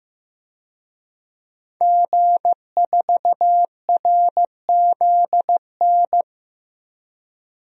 Morse code, heard as a keyed tone, about 15 wpm, 700 Hz. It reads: G4RZN
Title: DXCC-100 Call Signs - 5 Characters - 15wpm